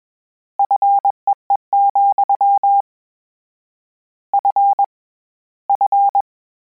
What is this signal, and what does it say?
Morse code, heard as a keyed tone, about 21 wpm, 800 Hz. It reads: FEE, F F
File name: Combinations / Combinations-mp3s / Combinations@20